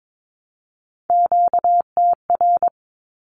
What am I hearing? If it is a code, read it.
Morse code, heard as a keyed tone, about 22 wpm, 700 Hz. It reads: QTR
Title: Q-Codes Repeated 3x 22wpm